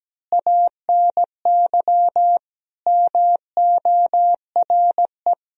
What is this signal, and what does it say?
Morse code, heard as a keyed tone, about 17 wpm, 700 Hz. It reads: ANY MORE
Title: Sentences from Top 1000 Words - Review Part 4 - 17wpm